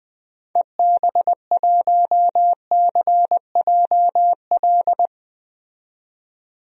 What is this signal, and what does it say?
Morse code, heard as a keyed tone, about 20 wpm, 700 Hz. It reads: EB1CJL
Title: DXCC-100 Call Signs - 6 Characters - 20wpm